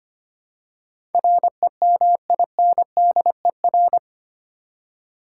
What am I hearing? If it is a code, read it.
Morse code, heard as a keyed tone, about 25 wpm, 700 Hz. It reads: REMINDER
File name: 8 Letter Words - First 3 Letters Given - 25wpm